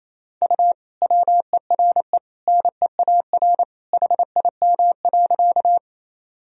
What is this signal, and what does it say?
Morse code, heard as a keyed tone, about 28 wpm, 700 Hz. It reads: U WERE NEAR HIM.
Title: Sentences from Top 200 Words and Abbreviations 28wpm